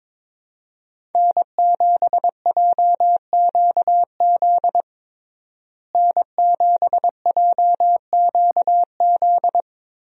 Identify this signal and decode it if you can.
Morse code, heard as a keyed tone, about 22 wpm, 700 Hz. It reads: N7JQZ N7JQZ
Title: USA Call Signs - 5 Characters - Repeated 3x - 22wpm